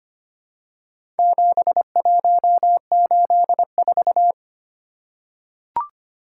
Morse code, heard as a keyed tone, about 25 wpm, 700 Hz. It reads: 7184
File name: practice / 25wpm / Four Letter-Number Combinations - 25wpm